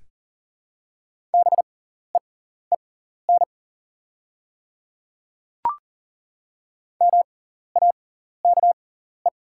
Morse code, heard as a keyed tone, about 40 wpm, spaced as for 14 wpm, 700 Hz. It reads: BEEN MAKE
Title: 6x Character Spacing - Top 100 Words - 40wpm